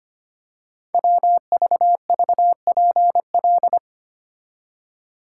Morse code, heard as a keyed tone, about 25 wpm, 700 Hz. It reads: WVVPL
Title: Code Groups - 5 Letter:Numbers - 25wpm